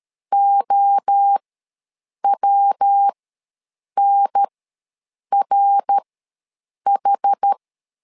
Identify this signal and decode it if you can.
Morse code, heard as a keyed tone, about 13 wpm, spaced as for 7 wpm, 800 Hz. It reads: OWNRH
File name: LEZIONI / 3 MWRO / 3 MWRO ws 12wpm 1914